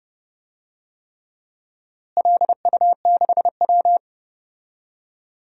Morse code, heard as a keyed tone, about 30 wpm, 700 Hz. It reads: LU6W
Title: DXCC-100 Call Signs - 4 Characters - Repeated 3x - 30wpm